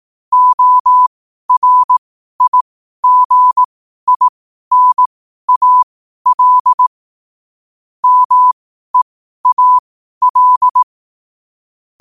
Morse code, heard as a keyed tone, about 18 wpm, spaced as for 12 wpm, 1000 Hz. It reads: ORIGINAL MEAL